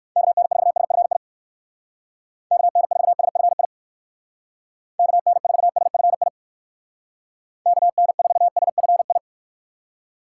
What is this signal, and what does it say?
Morse code, heard as a keyed tone, about 64 wpm, 700 Hz. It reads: KN4SFI KN4SFI KN4SFI KN4SFI